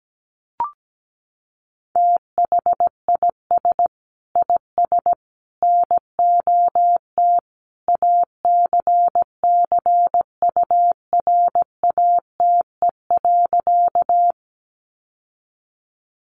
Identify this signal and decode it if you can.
Morse code, heard as a keyed tone, about 17 wpm, 700 Hz. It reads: THIS IS NOT ACCURATE.
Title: Sentences from Top 1200 Words 17wpm